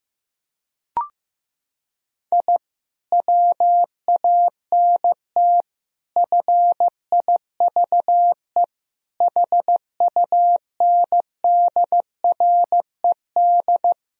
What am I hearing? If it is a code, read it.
Morse code, heard as a keyed tone, about 15 wpm, 700 Hz. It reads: I WANT FIVE HUNDRED
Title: Sentences from Top 1000 Words - Review Part 2 - 15wpm